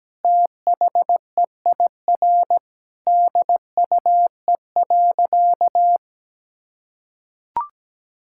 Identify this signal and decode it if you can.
Morse code, heard as a keyed tone, about 17 wpm, 700 Hz. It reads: THEIR DUE.
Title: Sentences from Top 1000 Words - Review Part 6 - 17wpm